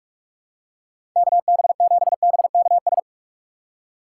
Morse code, heard as a keyed tone, about 42 wpm, 700 Hz. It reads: KB7BKS
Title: USA Call Signs - 2x Character Spacing - 45wpm